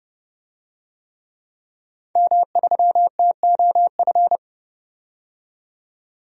Morse code, heard as a keyed tone, about 30 wpm, 700 Hz. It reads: M3TOF
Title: DXCC-100 Call Signs - 5 Characters - 30wpm